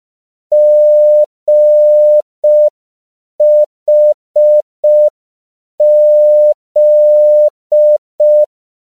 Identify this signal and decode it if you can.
Morse code, heard as a keyed tone, about 5 wpm, 600 Hz. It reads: GHZ
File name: CW / 5PPM